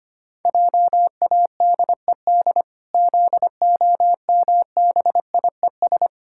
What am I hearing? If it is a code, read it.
Morse code, heard as a keyed tone, about 25 wpm, 700 Hz. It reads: JADED ZOMBIES
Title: Pangram Challenge v2 - 25wpm